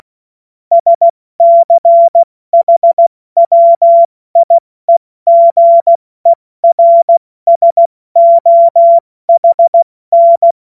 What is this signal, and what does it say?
Morse code, heard as a keyed tone, about 16 wpm, 680 Hz. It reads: SCHWIEGERSOHN